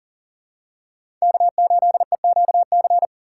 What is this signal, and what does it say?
Morse code, heard as a keyed tone, about 40 wpm, 700 Hz. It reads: K8EQC
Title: USA Call Signs - 5 Characters - 4x Character Spacing - 40wpm